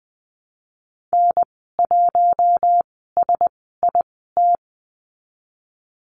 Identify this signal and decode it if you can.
Morse code, heard as a keyed tone, about 20 wpm, spaced as for 15 wpm, 700 Hz. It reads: N1SIT
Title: 042 Lesson 8g - Number 1 - Call Signs 20wpm